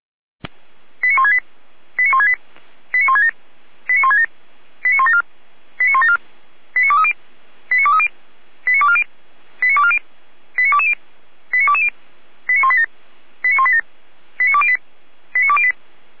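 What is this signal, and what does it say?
Five-tone selective call (ZVEI), heard as quick runs of five tones, tones of about 70 ms. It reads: 89168 89168 89167 89167 89174 89174 89230 89230 89240 89240 89220 89220 89187 89187 89208 89208
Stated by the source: ZVEI five-tone selective call 89115 received by SDR.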